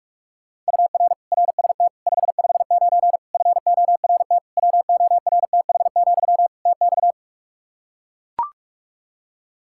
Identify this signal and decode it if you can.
Morse code, heard as a keyed tone, about 41 wpm, 700 Hz. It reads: UR RST 559 FORT WORTH, TX
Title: Practice QSO - Exchange 1 - Beginning - 45wpm